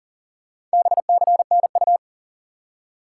Morse code, heard as a keyed tone, about 37 wpm, 700 Hz. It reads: BCNU
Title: Top 90 QSO Elements 40wpm